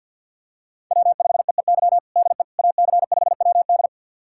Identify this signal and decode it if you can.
Morse code, heard as a keyed tone, about 52 wpm, 700 Hz. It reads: W5EEY DE AC5WD
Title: USA Callsign DE Callsign - 5 Characters - 50wpm